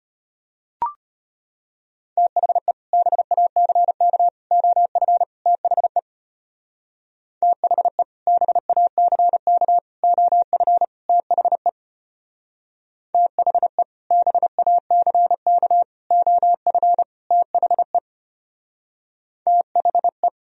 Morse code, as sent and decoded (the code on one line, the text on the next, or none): - .... / -... .- -.-. -.- / --- ..-. / - .... / - .... / -... .- -.-. -.- / --- ..-. / - .... / - .... / -... .- -.-. -.- / --- ..-. / - .... / - .... .
THE BACK OF THE THE BACK OF THE THE BACK OF THE THE